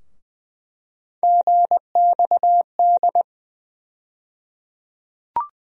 Morse code, heard as a keyed tone, about 21 wpm, 700 Hz. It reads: GXD
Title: USA Call Sign Postfixes - 3 Letters 20wpm